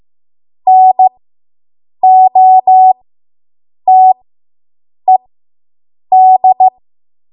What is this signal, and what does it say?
Morse code, heard as a keyed tone, about 15 wpm, spaced as for 7 wpm, 750 Hz. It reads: NOTED